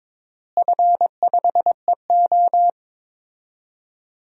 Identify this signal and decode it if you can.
Morse code, heard as a keyed tone, about 22 wpm, 700 Hz. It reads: F5EO
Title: DXCC-100 Call Signs - 4 Characters - Repeated 3x - 22wpm